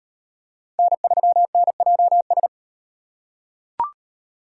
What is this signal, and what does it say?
Morse code, heard as a keyed tone, about 38 wpm, 700 Hz. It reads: N3NJS